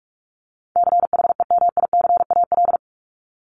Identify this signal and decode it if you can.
Morse code, heard as a keyed tone, about 44 wpm, 700 Hz. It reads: CHEMICAL
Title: Top 1001-1250 Words 45wpm